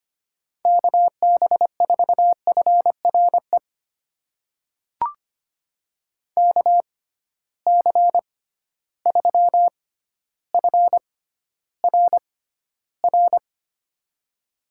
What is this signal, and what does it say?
Morse code, heard as a keyed tone, about 25 wpm, spaced as for 13 wpm, 700 Hz. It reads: KB4FRE KC3FRR
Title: USA Call Signs - 6 Characters - 6x Character Spacing - 25wpm